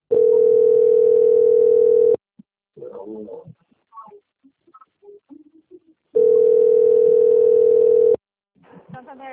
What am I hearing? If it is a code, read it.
Ringback tone (North America), 440 + 480 Hz, with speech.
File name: out- - - - - .63658